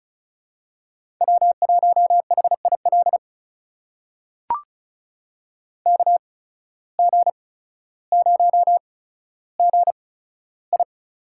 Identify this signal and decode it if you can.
Morse code, heard as a keyed tone, about 34 wpm, spaced as for 15 wpm, 690 Hz. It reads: W1HIL KG0GI
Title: USA Call Signs - 5 Characters - 8x Character Spacing - 35wpm